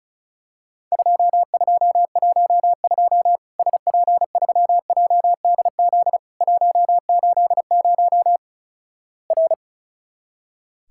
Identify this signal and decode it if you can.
Morse code, heard as a keyed tone, about 36 wpm, 700 Hz. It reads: 2212 SP3JDZ 180 R
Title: Worked All Europe DX Contest - QTC Exchange - 35wpm